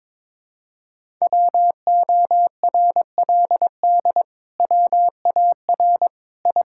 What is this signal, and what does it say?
Morse code, heard as a keyed tone, about 22 wpm, 700 Hz. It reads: WORLD WAR I
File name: practice / 22wpm / Sets of 7 Words - 22wpm